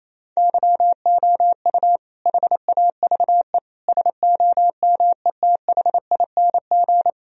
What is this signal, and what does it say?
Morse code, heard as a keyed tone, about 28 wpm, 700 Hz. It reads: YOU HAVE SOMETHING
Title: Sentences from Top 1000 Words - Review Part 4 - 28wpm